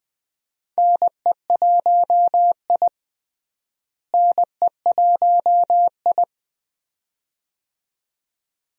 Morse code, heard as a keyed tone, about 20 wpm, 700 Hz. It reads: NE1I NE1I
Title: USA Call Signs - 4 Characters - Repeated 2x - 20wpm